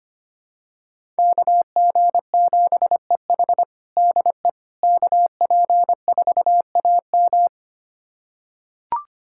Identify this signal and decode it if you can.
Morse code, heard as a keyed tone, about 25 wpm, 700 Hz. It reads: KG7EH DE KP4AM
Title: USA Callsign DE Callsign - 5 Characters - 25wpm